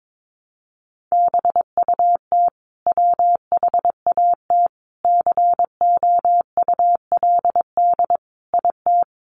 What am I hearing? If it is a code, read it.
Morse code, heard as a keyed tone, about 22 wpm, 700 Hz. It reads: BUT WHAT COULD IT
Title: Sentences from Top 1300 Words 22wpm